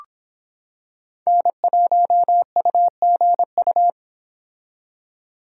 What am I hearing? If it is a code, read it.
Morse code, heard as a keyed tone, about 26 wpm, 700 Hz. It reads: N1UGU